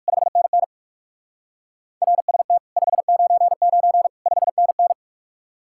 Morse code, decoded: 5NN RST 599 5NN

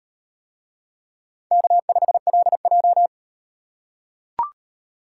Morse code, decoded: K5LJ

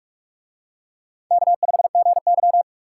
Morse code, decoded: K5GY

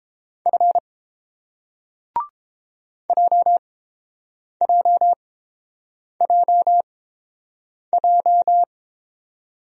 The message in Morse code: F JJJJ

